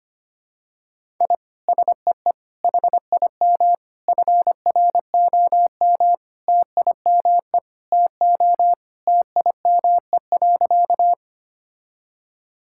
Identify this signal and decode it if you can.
Morse code, heard as a keyed tone, about 25 wpm, 700 Hz. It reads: I SEE HIM FROM TIME TO TIME.
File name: practice / 25wpm / Sentences with 4-Word Phrases - 25wpm